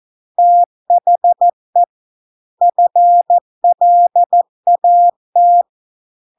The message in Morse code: THE FLAT

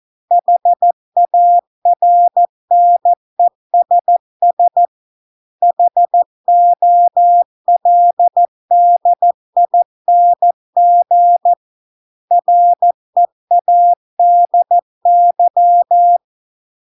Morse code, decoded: HARNESS HOLDING READY

